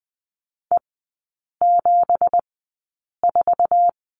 Morse code, heard as a keyed tone, about 20 wpm, spaced as for 8 wpm, 700 Hz. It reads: E74
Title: 166 Lesson 27e - Number 7 - Three Characters 20wpm